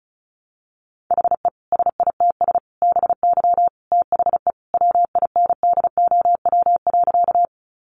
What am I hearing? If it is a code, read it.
Morse code, heard as a keyed tone, about 34 wpm, 700 Hz. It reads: HE SITS BY THE WINDOW.
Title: Easy Phrases 35wpm